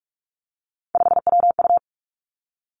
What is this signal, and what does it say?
Morse code, heard as a keyed tone, about 46 wpm, 700 Hz. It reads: 5WU
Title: Three Challenging Letter-Number Combinations 45wpm